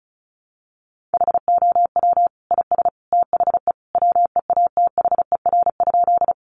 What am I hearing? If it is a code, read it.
Morse code, heard as a keyed tone, about 34 wpm, 700 Hz. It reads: HOW IS THE WEATHER?
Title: Sentences from Top 800 Words 35wpm